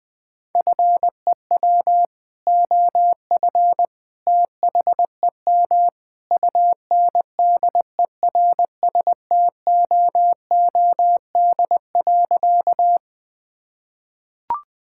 Morse code, heard as a keyed tone, about 20 wpm, 700 Hz. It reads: FEW OF THEM UNDERSTOOD.
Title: Sentences with 2-Word Phrases - 20wpm